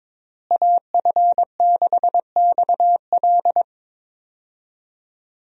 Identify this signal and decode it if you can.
Morse code, heard as a keyed tone, about 22 wpm, 700 Hz. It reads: AF6XL
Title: USA Call Signs - 5 Characters - Repeated 3x - 22wpm